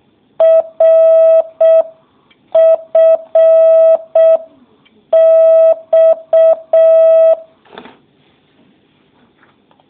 Morse code, heard as a keyed tone, about 6 wpm, 650 Hz. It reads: RFX